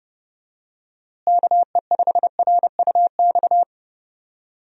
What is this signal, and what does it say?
Morse code, heard as a keyed tone, about 30 wpm, 700 Hz. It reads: KE5RUX
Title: USA Call Signs - 6 Characters - Repeated 2x - 30wpm